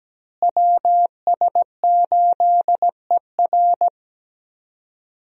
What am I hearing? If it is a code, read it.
Morse code, heard as a keyed tone, about 17 wpm, 700 Hz. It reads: WS8ER
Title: USA Call Signs - 5 Characters - Repeated 3x - 17wpm